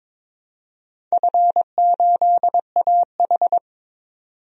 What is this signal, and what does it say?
Morse code, heard as a keyed tone, about 22 wpm, 700 Hz. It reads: F8AH
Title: DXCC-100 Call Signs - 4 Characters - 22wpm